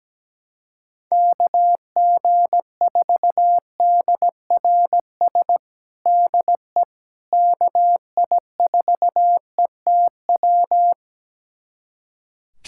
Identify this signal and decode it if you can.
Morse code, heard as a keyed tone, about 17 wpm, 700 Hz. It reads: KG4DRS DE KI4ETW
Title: USA Callsign DE Callsign - 6 Characters - 17wpm